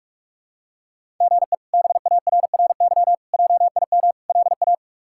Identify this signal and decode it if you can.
Morse code, heard as a keyed tone, about 42 wpm, 700 Hz. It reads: GE BARRY JIM LA
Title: K1USN SST QSO Practice - 45wpm